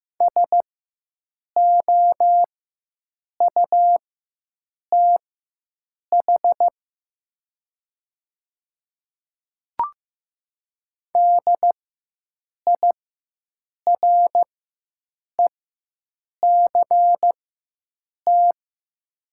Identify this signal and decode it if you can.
Morse code, heard as a keyed tone, about 15 wpm, spaced as for 7 wpm, 700 Hz. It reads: SOUTH DIRECT
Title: 4x Character Spacing - Top 500 Words - 15wpm